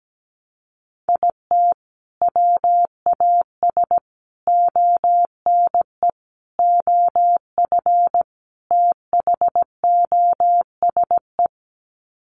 Morse code, heard as a keyed tone, about 17 wpm, 700 Hz. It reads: IT WAS ONE OF THOSE
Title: Sets of 5 Words Encore 17wpm